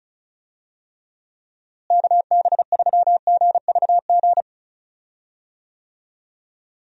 Morse code, heard as a keyed tone, about 34 wpm, 700 Hz. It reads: KB3GVG
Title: USA Call Signs - 6 Characters - Repeated 3x - 35wpm